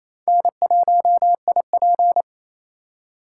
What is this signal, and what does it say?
Morse code, heard as a keyed tone, about 34 wpm, 700 Hz. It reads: N1IP
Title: Speed Racing - USA Call Signs 28wpm